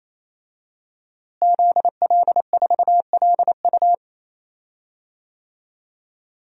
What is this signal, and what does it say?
Morse code, heard as a keyed tone, about 28 wpm, 700 Hz. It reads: ZL4LU
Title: DXCC-100 Call Signs - 5 Characters - Repeated 3x - 28wpm